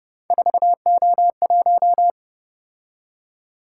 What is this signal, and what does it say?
Morse code, heard as a keyed tone, about 28 wpm, 700 Hz. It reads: 4O1